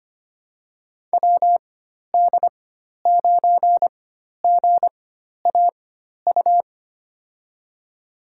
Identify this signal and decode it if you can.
Morse code, heard as a keyed tone, about 25 wpm, spaced as for 12 wpm, 700 Hz. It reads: WD9GAU
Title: USA Call Signs - 6 Characters - 4x Character Spacing - 25wpm